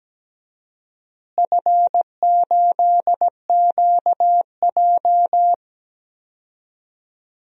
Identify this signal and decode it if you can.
Morse code, heard as a keyed tone, about 17 wpm, 700 Hz. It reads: F8QJ
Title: DXCC-100 Call Signs - 4 Characters - Repeated 3x - 17wpm